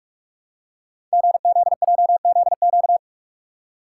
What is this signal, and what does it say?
Morse code, heard as a keyed tone, about 44 wpm, 700 Hz. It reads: GZJZQ
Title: Code Groups - 5 Letter:Numbers - 45wpm